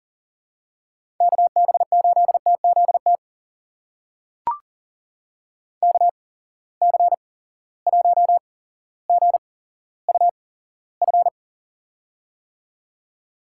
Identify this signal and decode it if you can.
Morse code, heard as a keyed tone, about 40 wpm, spaced as for 17 wpm, 700 Hz. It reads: KB8TZT KC1GUF
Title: USA Call Signs - 6 Characters - 8x Character Spacing - 40wpm